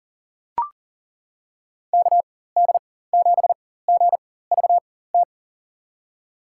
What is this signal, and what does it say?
Morse code, heard as a keyed tone, about 40 wpm, spaced as for 19 wpm, 700 Hz. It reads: KD7GVT
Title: USA Call Signs - 4x Character Spacing - 40wpm